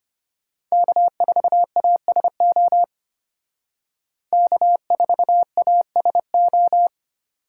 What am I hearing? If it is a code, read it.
Morse code, heard as a keyed tone, about 32 wpm, 700 Hz. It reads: K4ASO K4ASO